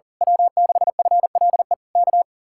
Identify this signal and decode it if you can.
Morse code, heard as a keyed tone, about 38 wpm, 700 Hz. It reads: W6FLE K
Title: Speed-Racing Common QSO Phrases - 40wpm